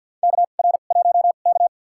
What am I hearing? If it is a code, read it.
Morse code, heard as a keyed tone, about 49 wpm, spaced as for 35 wpm, 700 Hz. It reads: KR1K
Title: USA Call Signs - 4 Characters - 2x Character Spacing - 50wpm